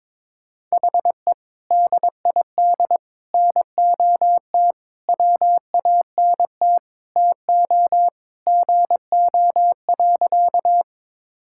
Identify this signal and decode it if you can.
Morse code, heard as a keyed tone, about 22 wpm, 700 Hz. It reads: HE DID NOT WANT TO GO.
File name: practice / 22wpm / Sentences with 3-Word Phrases - 22wpm